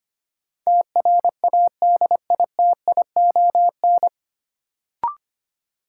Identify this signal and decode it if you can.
Morse code, heard as a keyed tone, about 25 wpm, 700 Hz. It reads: TRADITION
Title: Sentences from Top 1500 Words - New Word Drill Alt - 25wpm